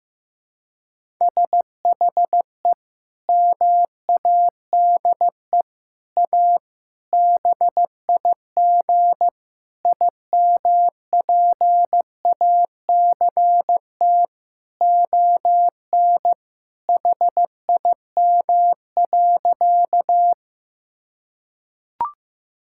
Morse code, heard as a keyed tone, about 15 wpm, 700 Hz. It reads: SHE MADE A BIG IMPACT ON HIM.
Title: Sentences from Top 1000 Words - Review Part 6 - 15wpm